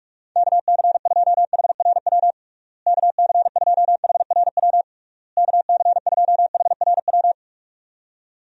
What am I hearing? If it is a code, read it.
Morse code, heard as a keyed tone, about 48 wpm, 700 Hz. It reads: KC2HRW KC2HRW KC2HRW
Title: USA Call Signs - 6 Characters - Repeated 3x - 45wpm